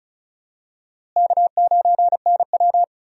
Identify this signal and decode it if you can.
Morse code, heard as a keyed tone, about 36 wpm, 700 Hz. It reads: K9NW
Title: USA Call Signs - 4 Characters - Repeated 2x - 35wpm